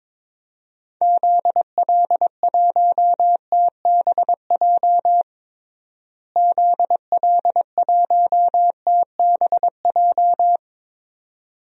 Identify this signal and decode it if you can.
Morse code, heard as a keyed tone, about 22 wpm, 700 Hz. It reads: ZL1TBJ ZL1TBJ
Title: DXCC-100 Call Signs - 6 Characters - Repeated 2x - 22wpm